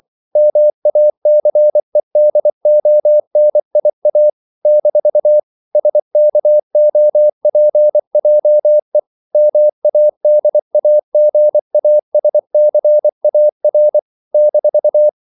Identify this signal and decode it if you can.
Morse code, heard as a keyed tone, about 24 wpm, 600 Hz. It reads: MACEDONIA - SKOPJE MADAGASCAR -
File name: CW / Practice / NationalCapitals24